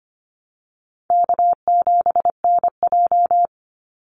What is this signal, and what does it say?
Morse code, heard as a keyed tone, about 25 wpm, 700 Hz. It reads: K7NJ
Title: Easy Call Signs Repeated 3x - 25wpm